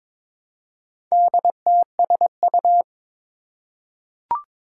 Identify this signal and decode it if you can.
Morse code, heard as a keyed tone, about 22 wpm, 700 Hz. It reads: DTSU
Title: Four Challenging Letter-Number Combinations 22wpm